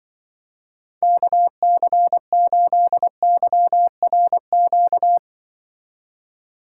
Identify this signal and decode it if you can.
Morse code, heard as a keyed tone, about 24 wpm, 700 Hz. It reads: KC8YRQ
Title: Speed Racing - USA 2x3 Call Signs 20wpm